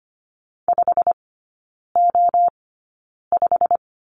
Morse code, heard as a keyed tone, about 25 wpm, spaced as for 9 wpm, 690 Hz. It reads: 5O5
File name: Lesson 15e - Number 5 - Three Characters 25wpm